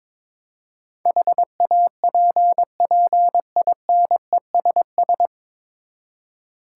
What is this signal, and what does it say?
Morse code, heard as a keyed tone, about 22 wpm, 700 Hz. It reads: HAPPINESS
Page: Morse Code Ninja practice 22wpm